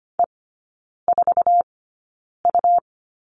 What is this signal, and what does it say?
Morse code, heard as a keyed tone, about 25 wpm, spaced as for 9 wpm, 700 Hz. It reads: E4U
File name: Lesson 17e - Letter u - Three Characters 25wpm